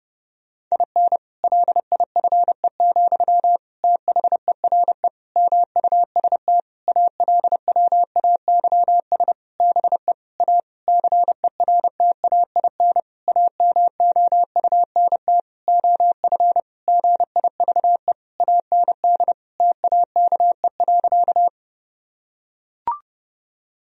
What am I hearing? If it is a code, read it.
Morse code, heard as a keyed tone, about 30 wpm, 700 Hz. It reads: IN LIFE, THERE MUST ALWAYS BE A CERTAIN AMOUNT OF GIVE AND TAKE.
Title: Sentences from Top 1000 Words - Review Part 4 - 30wpm